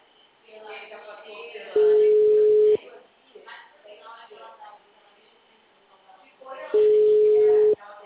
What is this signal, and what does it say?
Ringback tone (Europe, 425 Hz), with speech.
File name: out- -2025- - - .195